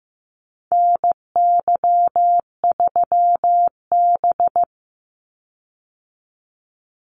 Morse code, heard as a keyed tone, about 15 wpm, 700 Hz. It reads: NY3B
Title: Call Signs Repeated 3x - 15wpm